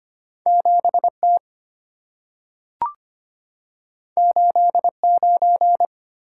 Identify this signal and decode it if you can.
Morse code, heard as a keyed tone, about 25 wpm, 700 Hz. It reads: 7T 89
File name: Two Letter-Number Combinations - 25wpm